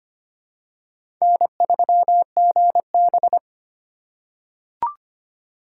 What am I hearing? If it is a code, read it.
Morse code, heard as a keyed tone, about 25 wpm, 700 Hz. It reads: N3GB
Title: USA Call Signs - 4 Characters - 4x Character Spacing - 25wpm